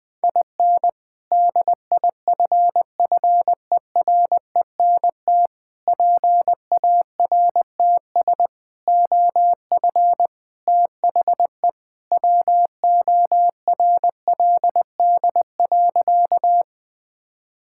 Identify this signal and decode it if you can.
Morse code, heard as a keyed tone, about 20 wpm, 700 Hz. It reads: IN DIFFERENT PARTS OF THE WORLD.
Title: Sentences with 3-Word Phrases - 20wpm